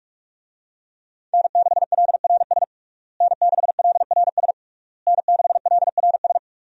Morse code, heard as a keyed tone, about 41 wpm, 700 Hz. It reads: N6LRS N6LRS N6LRS
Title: USA Call Signs - 5 Characters - Repeated 3x - 45wpm